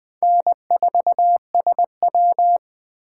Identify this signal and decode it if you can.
Morse code, heard as a keyed tone, about 20 wpm, 700 Hz. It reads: N4SW